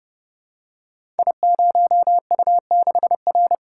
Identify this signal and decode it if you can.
Morse code, heard as a keyed tone, about 30 wpm, 700 Hz. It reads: I0U6R
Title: Code Groups - 5 Letter:Numbers - 30wpm